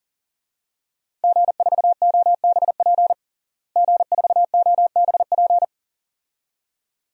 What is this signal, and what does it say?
Morse code, heard as a keyed tone, about 40 wpm, 700 Hz. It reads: G4OBP G4OBP
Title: DXCC-100 Call Signs - 5 Characters - Repeated 2x - 40wpm